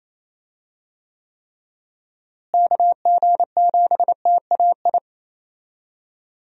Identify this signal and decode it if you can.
Morse code, heard as a keyed tone, about 27 wpm, 700 Hz. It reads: KG7TAI
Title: USA Call Signs - 6 Characters - Repeated 3x - 28wpm